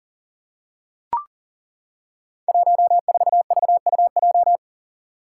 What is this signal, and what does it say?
Morse code, heard as a keyed tone, about 41 wpm, 690 Hz. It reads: 14VUJ
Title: Five Challenging Letter-Number Combinations 40wpm